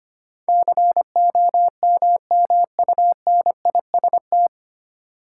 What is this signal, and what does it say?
Morse code, heard as a keyed tone, about 25 wpm, 700 Hz. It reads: COMMUNIST
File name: practice / 25wpm / 9 Letter Words - First 3 Letters Given - 25wpm